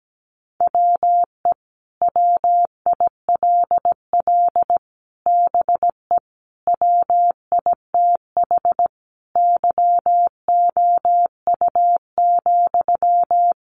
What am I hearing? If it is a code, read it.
Morse code, heard as a keyed tone, about 17 wpm, 700 Hz. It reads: WE WILL BE WITH YOU,
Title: Sentences from Top 600 Words 17wpm